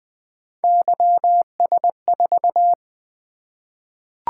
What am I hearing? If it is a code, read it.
Morse code, heard as a keyed tone, about 20 wpm, 700 Hz. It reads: YS4
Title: Three Letter-Number Combinations - 20wpm